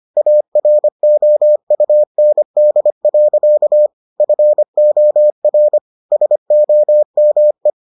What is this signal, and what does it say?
Morse code, heard as a keyed tone, about 25 wpm, 600 Hz. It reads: AROUND. FOR SOME